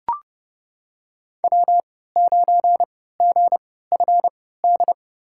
Morse code, heard as a keyed tone, about 30 wpm, spaced as for 17 wpm, 700 Hz. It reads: W9GFD
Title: USA Call Signs - 5 Characters - 3x Character Spacing - 30wpm